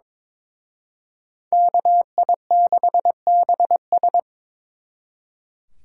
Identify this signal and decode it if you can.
Morse code, heard as a keyed tone, about 26 wpm, 700 Hz. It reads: KI6BS
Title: Speed Racing - USA 2x2 Call Signs 22wpm